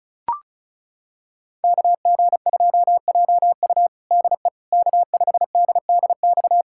Morse code, heard as a keyed tone, about 36 wpm, 700 Hz. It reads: KG2JU DE K5DDX
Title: USA Callsign DE Callsign - 5 Characters - 35wpm